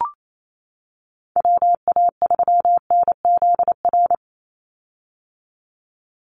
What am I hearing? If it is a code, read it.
Morse code, heard as a keyed tone, about 28 wpm, 700 Hz. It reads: WA3NZR
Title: Call Signs - 28wpm